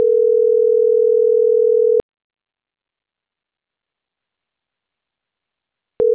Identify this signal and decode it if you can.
Ringback tone (North America), 440 + 480 Hz.